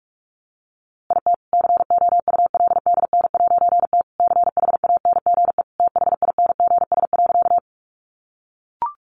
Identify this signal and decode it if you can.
Morse code, heard as a keyed tone, about 44 wpm, 700 Hz. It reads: IT COULDN'T CHANGE THINGS.
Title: A Sound Of Thunder 45wpm